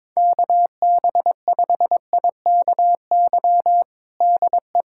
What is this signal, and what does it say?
Morse code, heard as a keyed tone, about 22 wpm, 700 Hz. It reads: KB5IKY DE
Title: USA Callsign DE Callsign - 6 Characters - 22wpm